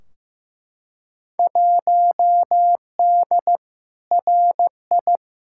Morse code, heard as a keyed tone, about 15 wpm, 700 Hz. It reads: 1D RI
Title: Field Day Exchanges - 15wpm